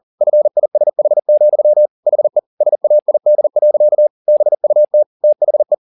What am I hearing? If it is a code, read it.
Morse code, heard as a keyed tone, about 40 wpm, 600 Hz. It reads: FISH, HE SAID. BUT THE
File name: the old man and the sea 2 40wpm